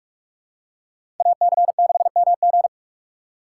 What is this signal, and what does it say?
Morse code, heard as a keyed tone, about 46 wpm, 700 Hz. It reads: AC6MG
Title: USA Call Signs - 5 Characters - 6x Character Spacing - 45wpm